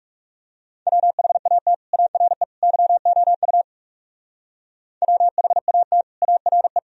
Morse code, heard as a keyed tone, about 37 wpm, 700 Hz. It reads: WHAT ARE YOU WHAT ARE
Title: Speed-Racing - Sets of 3 Words 30wpm